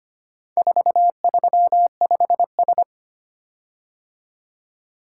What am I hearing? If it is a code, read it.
Morse code, heard as a keyed tone, about 25 wpm, 700 Hz. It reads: 435S
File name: practice / 25wpm / Four Letter-Number Combinations - 25wpm